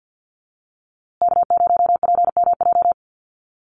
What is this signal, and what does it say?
Morse code, heard as a keyed tone, about 50 wpm, 700 Hz. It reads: K0PMJ
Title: USA Call Signs Repeated 3x 50wpm